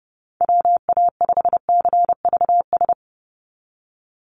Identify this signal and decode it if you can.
Morse code, heard as a keyed tone, about 30 wpm, 700 Hz. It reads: WA5CVS
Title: USA Call Signs 30wpm